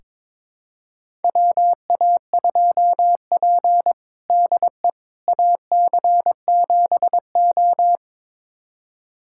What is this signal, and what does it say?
Morse code, heard as a keyed tone, about 22 wpm, 700 Hz. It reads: WA2P DE AC7O
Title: USA Callsign DE Callsign - 4 Characters - 22wpm